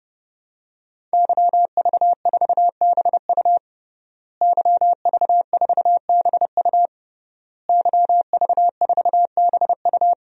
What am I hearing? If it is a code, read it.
Morse code, heard as a keyed tone, about 29 wpm, 700 Hz. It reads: YV4BU YV4BU YV4BU
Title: DXCC-100 Call Signs - 5 Characters - Repeated 3x - 30wpm